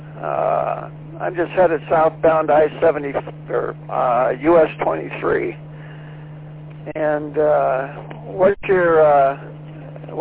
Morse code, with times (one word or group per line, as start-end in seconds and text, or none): none